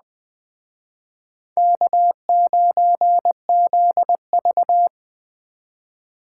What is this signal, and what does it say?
Morse code, heard as a keyed tone, about 19 wpm, 700 Hz. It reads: K9ZV